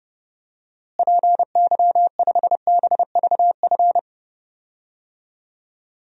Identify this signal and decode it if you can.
Morse code, heard as a keyed tone, about 30 wpm, 700 Hz. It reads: PY5BVF
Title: DXCC-100 Call Signs - 6 Characters - Repeated 3x - 30wpm